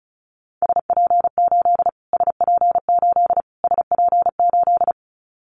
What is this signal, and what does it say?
Morse code, heard as a keyed tone, about 34 wpm, 700 Hz. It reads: SP8 SP8 SP8
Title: Callsign Prefixes for Top 25 DXCC - Repeated 3x - 35wpm